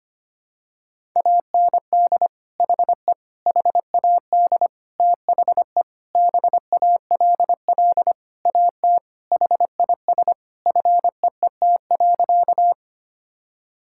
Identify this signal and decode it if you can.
Morse code, heard as a keyed tone, about 25 wpm, 700 Hz. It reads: AND HE HAD THE BALL AT HIS FEET.
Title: Sentences from Top 1000 Words - Review Part 6 - 25wpm